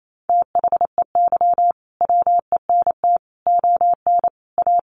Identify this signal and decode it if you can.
Morse code, heard as a keyed tone, about 28 wpm, 700 Hz. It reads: THEY WENT ON A
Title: Sentences from Top 1400 Words 28wpm